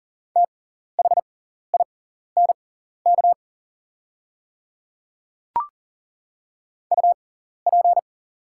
Morse code, heard as a keyed tone, about 38 wpm, spaced as for 14 wpm, 700 Hz. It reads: THINK UP
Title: 6x Character Spacing - Top 200 Words - 40wpm